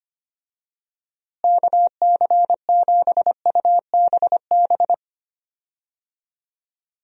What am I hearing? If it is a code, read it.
Morse code, heard as a keyed tone, about 25 wpm, 700 Hz. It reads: KC7UBB
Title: USA Call Signs - 6 Characters - Repeated 2x - 25wpm